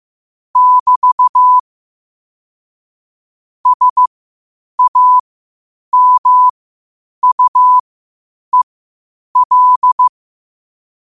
Morse code, heard as a keyed tone, about 15 wpm, spaced as for 8 wpm, 1000 Hz. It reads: <BT> SAMUEL
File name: qotdfiles / 8wpm